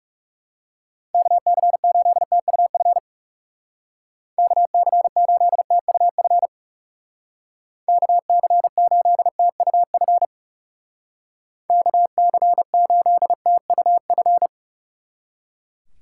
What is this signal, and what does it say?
Morse code, heard as a keyed tone, about 37 wpm, 700 Hz. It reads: KC8TUF KC8TUF KC8TUF KC8TUF
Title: Speed Racing - USA 2x3 Call Signs 30wpm